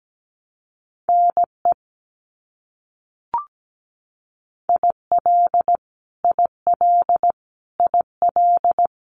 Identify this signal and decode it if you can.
Morse code, heard as a keyed tone, about 17 wpm, 700 Hz. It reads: NE IL IL IL
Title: US States Repeated 3x 17wpm